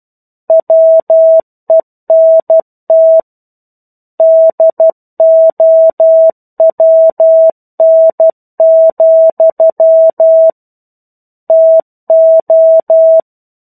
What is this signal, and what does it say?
Morse code, heard as a keyed tone, about 12 wpm, 650 Hz. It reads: WENT DOWN, TO
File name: 01-Chapter-01 12 WPM